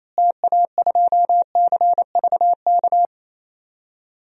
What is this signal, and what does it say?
Morse code, heard as a keyed tone, about 28 wpm, 700 Hz. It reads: TA2CVK
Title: DXCC-100 Call Signs - 6 Characters - Repeated 3x - 28wpm